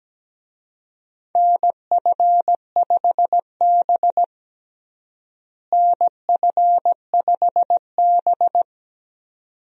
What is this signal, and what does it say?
Morse code, heard as a keyed tone, about 17 wpm, 700 Hz. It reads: NF5B NF5B
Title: USA Call Signs - 4 Characters - Repeated 2x - 17wpm